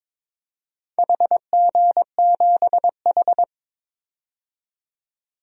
Morse code, heard as a keyed tone, about 22 wpm, 700 Hz. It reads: HG7H
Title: DXCC-100 Call Signs - 4 Characters - 22wpm